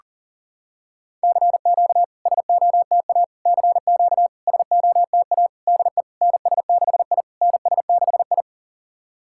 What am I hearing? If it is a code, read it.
Morse code, heard as a keyed tone, about 40 wpm, 690 Hz. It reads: CQ SOTA CQ SOTA DE NS6I NS6I
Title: SOTA QSO Practice 40wpm